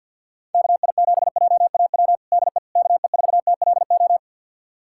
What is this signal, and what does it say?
Morse code, heard as a keyed tone, about 49 wpm, 700 Hz. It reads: KI7JAW DE KE4TLO
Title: USA Callsign DE Callsign 50wpm